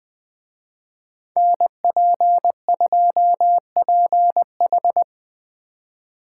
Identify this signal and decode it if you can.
Morse code, heard as a keyed tone, about 24 wpm, 700 Hz. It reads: NP2PH
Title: Speed Racing - USA 2x2 Call Signs 20wpm